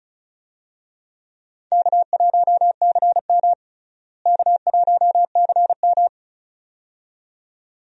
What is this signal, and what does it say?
Morse code, heard as a keyed tone, about 36 wpm, 700 Hz. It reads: K1CM K1CM
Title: USA Call Signs - 4 Characters - Repeated 2x - 35wpm